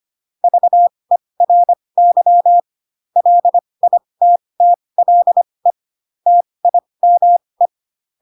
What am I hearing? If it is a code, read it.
Morse code, heard as a keyed tone, about 25 wpm, spaced as for 20 wpm, 700 Hz. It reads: VERY LITTLE TIME
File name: Pinocchio / Pinocchio ch4 15 25 ik1hge